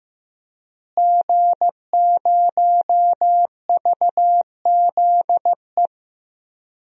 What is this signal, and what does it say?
Morse code, heard as a keyed tone, about 15 wpm, 700 Hz. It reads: G0VZE
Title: DXCC-100 Call Signs - 5 Characters - Repeated 3x - 15wpm